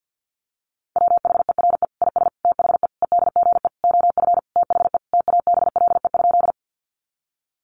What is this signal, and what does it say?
Morse code, heard as a keyed tone, about 50 wpm, 700 Hz. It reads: WHERE IS THE EDGE OF THE TABLE?
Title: Sentences from Top 1300 Words 50wpm